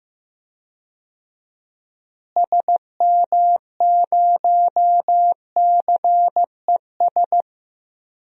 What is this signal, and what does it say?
Morse code, heard as a keyed tone, about 15 wpm, 700 Hz. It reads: SM0CES
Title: DXCC-100 Call Signs - 6 Characters - Repeated 2x - 15wpm